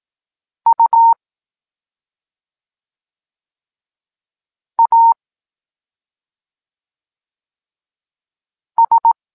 Morse code, heard as a keyed tone, about 18 wpm, 920 Hz. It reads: U A S
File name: Tekst-23 4